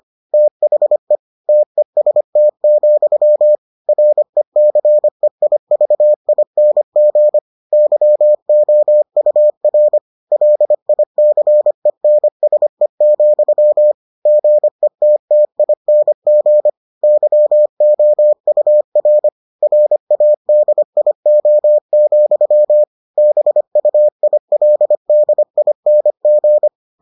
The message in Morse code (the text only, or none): THE TEST, RECEIVING YOUR LICENSE, GETTING YOUR RADIO, BUILDING